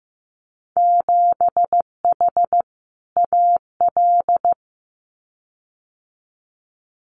Morse code, heard as a keyed tone, about 15 wpm, 700 Hz. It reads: 7H AL
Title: Winter Field Day Exchanges Encore 15wpm